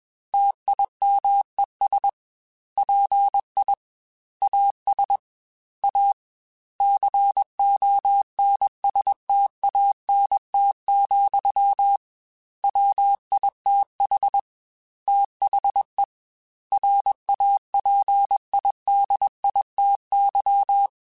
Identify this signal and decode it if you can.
Morse code, heard as a keyed tone, about 21 wpm, 800 Hz. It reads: TIMES PI AS A CONSTANT, WITH THE RAPIDITY
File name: Technical Subjects / Technical Subjects-mp3s / XL @20